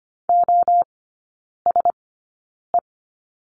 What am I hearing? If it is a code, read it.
Morse code, heard as a keyed tone, about 25 wpm, spaced as for 9 wpm, 700 Hz. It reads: OSE
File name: course / 25:10wpm / 093 Lesson 16e - Letter c - Three Characters 25wpm